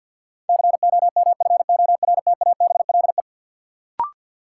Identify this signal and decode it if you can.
Morse code, heard as a keyed tone, about 52 wpm, 700 Hz. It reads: COMFORTABLE
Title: Sentences from Top 1500 Words - New Word Drill Alt - 50wpm